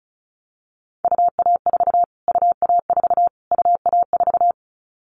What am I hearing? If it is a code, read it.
Morse code, heard as a keyed tone, about 34 wpm, 700 Hz. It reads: UA4 UA4 UA4
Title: Callsign Prefixes for Top 25 DXCC - Repeated 3x - 35wpm